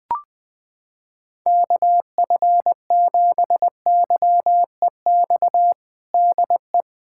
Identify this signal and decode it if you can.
Morse code, heard as a keyed tone, about 20 wpm, 700 Hz. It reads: KF7YEX DE
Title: USA Callsign DE Callsign 20wpm